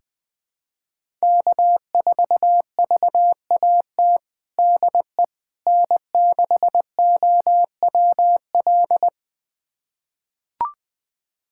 Morse code, heard as a keyed tone, about 20 wpm, 700 Hz. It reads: K4VAT DE N6OWL
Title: USA Callsign DE Callsign - 5 Characters - 20wpm